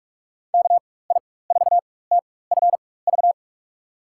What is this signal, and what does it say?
Morse code, heard as a keyed tone, about 44 wpm, spaced as for 21 wpm, 700 Hz. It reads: KI4TFV
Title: USA Call Signs - 6 Characters - 4x Character Spacing - 45wpm